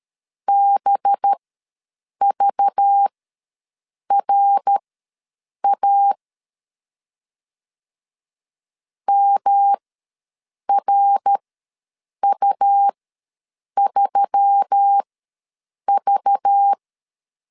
Morse code, heard as a keyed tone, about 13 wpm, spaced as for 7 wpm, 800 Hz. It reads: BVRA MRU3V
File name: LEZIONI / 5 LBVC / 5 LBV ws 12wpm 3015